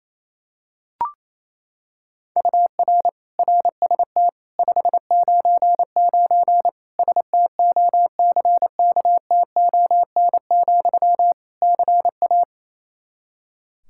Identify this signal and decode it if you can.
Morse code, heard as a keyed tone, about 28 wpm, 700 Hz. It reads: UR RST 599 STOCKTON, CA
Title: K1USN SST QSO Practice 28wpm